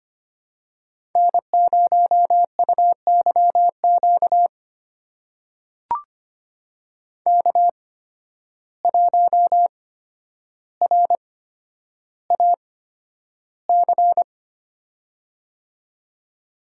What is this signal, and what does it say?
Morse code, heard as a keyed tone, about 25 wpm, spaced as for 11 wpm, 700 Hz. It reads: N0UYQ K1RAC
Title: USA Call Signs - 5 Characters - 8x Character Spacing - 25wpm